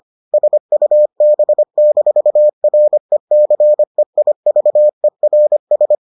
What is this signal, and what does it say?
Morse code, heard as a keyed tone, about 25 wpm, 600 Hz. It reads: SUB-RECEIVERS